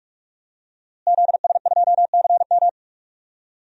Morse code, heard as a keyed tone, about 47 wpm, 700 Hz. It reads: ZS2CM
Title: DXCC-100 Call Signs - 5 Characters - Repeated 2x - 45wpm